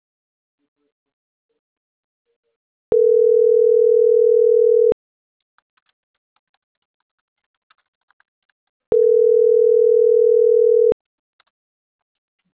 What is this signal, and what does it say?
Ringback tone (North America), 440 + 480 Hz.
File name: CENTRAL MAYAB / out fernanda.franco